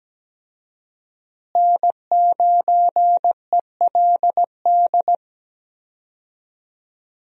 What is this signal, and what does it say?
Morse code, heard as a keyed tone, about 17 wpm, 700 Hz. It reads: N9ELD
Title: USA Call Signs - 5 Characters - Repeated 2x - 17wpm